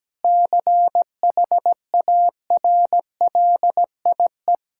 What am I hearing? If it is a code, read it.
Morse code, heard as a keyed tone, about 17 wpm, 700 Hz. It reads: CHARLIE
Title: K1USN SST QSO Practice - 17wpm